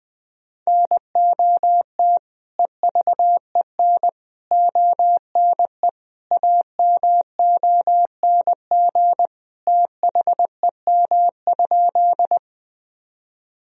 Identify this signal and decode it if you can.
Morse code, heard as a keyed tone, about 20 wpm, 700 Hz. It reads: NOT EVEN ONE AMONG THEM?
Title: Sentences from Top 1000 Words - Review Part 3 - 20wpm